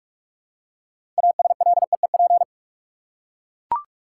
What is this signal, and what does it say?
Morse code, heard as a keyed tone, about 44 wpm, 700 Hz. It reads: ASLEEP
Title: Six Letter Words - 45wpm